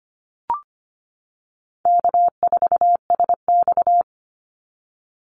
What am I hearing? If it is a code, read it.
Morse code, heard as a keyed tone, about 25 wpm, 700 Hz. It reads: K4SX
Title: Call Signs - 25wpm